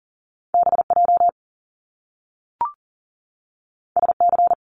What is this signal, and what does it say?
Morse code, heard as a keyed tone, about 41 wpm, 700 Hz. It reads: BJ SC
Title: Top Two Letter Combinations in Words 40wpm